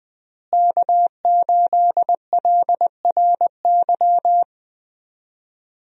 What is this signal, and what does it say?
Morse code, heard as a keyed tone, about 20 wpm, 700 Hz. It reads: K8LRY
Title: USA Call Signs - 5 Characters - Repeated 3x - 20wpm